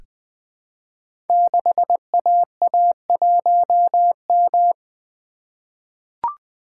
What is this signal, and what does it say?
Morse code, heard as a keyed tone, about 20 wpm, 700 Hz. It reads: 6AA1M
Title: Code Groups - 5 Letter:Numbers - 20wpm